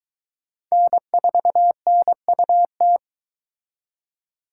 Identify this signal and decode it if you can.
Morse code, heard as a keyed tone, about 23 wpm, 700 Hz. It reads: N4NUT